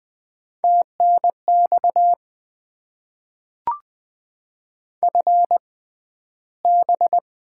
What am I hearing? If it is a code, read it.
Morse code, heard as a keyed tone, about 20 wpm, spaced as for 12 wpm, 700 Hz. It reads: TNX FB
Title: Top 20 QSO Elements - 6x Character Spacing - 20wpm